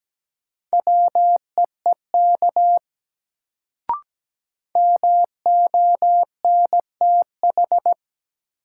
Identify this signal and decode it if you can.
Morse code, heard as a keyed tone, about 17 wpm, 700 Hz. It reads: WEEK MONTH
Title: Sets of 5 Words Encore - New Word Drill - 17wpm